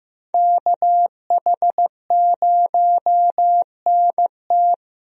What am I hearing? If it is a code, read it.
Morse code, heard as a keyed tone, about 15 wpm, 700 Hz. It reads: KH0NT
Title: USA Call Signs - 5 Characters - Repeated 2x - 15wpm